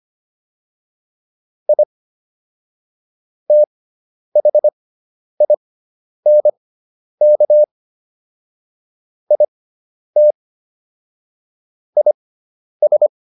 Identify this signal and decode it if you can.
Morse code, heard as a keyed tone, about 25 wpm, spaced as for 10 wpm, 600 Hz. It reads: I THINK IT IS